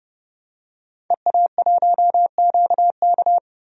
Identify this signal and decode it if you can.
Morse code, heard as a keyed tone, about 30 wpm, 700 Hz. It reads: EA1QK
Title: DXCC-100 Call Signs - 5 Characters - Repeated 3x - 30wpm